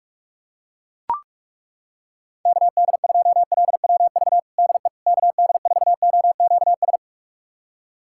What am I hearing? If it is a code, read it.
Morse code, heard as a keyed tone, about 41 wpm, 700 Hz. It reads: KD2LWV DE KD4OQS
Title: USA Callsign DE Callsign - 6 Characters - 45wpm